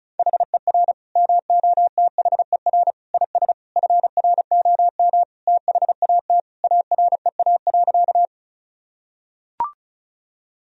Morse code, heard as a keyed tone, about 36 wpm, 700 Hz. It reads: HER MOTHER IS FROM THAT AREA.
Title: Sentences from Top 1000 Words - Review Part 2 - 35wpm